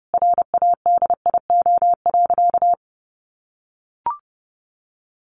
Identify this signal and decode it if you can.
Morse code, heard as a keyed tone, about 30 wpm, 700 Hz. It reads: RADIO.
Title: A Sound Of Thunder 30wpm